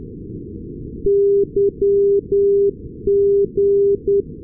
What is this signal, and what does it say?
Morse code, heard as a keyed tone, about 10 wpm, 400 Hz. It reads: YG
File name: europe / nor / nor-258 yg-396 la6lu-2020-12-kiwi regional-rb